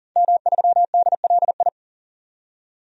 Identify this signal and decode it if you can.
Morse code, heard as a keyed tone, about 40 wpm, 700 Hz. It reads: M3DLI
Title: DXCC-100 Call Signs - 5 Characters - 40wpm